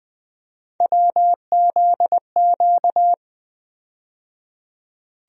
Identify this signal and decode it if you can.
Morse code, heard as a keyed tone, about 20 wpm, 700 Hz. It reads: WZQ
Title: USA Call Sign Postfixes - 3 Letters 20wpm